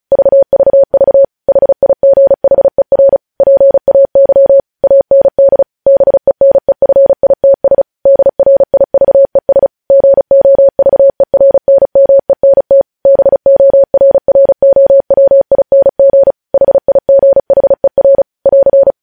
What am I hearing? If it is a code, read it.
Morse code, heard as a keyed tone, about 36 wpm, 550 Hz. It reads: VVV HIGHER PAY AND BENEFITS DRIVES GOVERNMENT BORROWING HIGHER <AR>
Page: News Headlines in Morse Code at 35 WPM